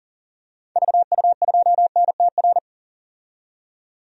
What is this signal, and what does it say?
Morse code, heard as a keyed tone, about 40 wpm, 700 Hz. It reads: VU2NTR
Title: DXCC-100 Call Signs - 6 Characters - 40wpm